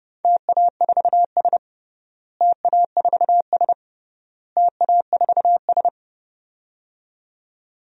Morse code, heard as a keyed tone, about 30 wpm, 700 Hz. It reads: TA4S TA4S TA4S
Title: DXCC-100 Call Signs - 4 Characters - Repeated 3x - 30wpm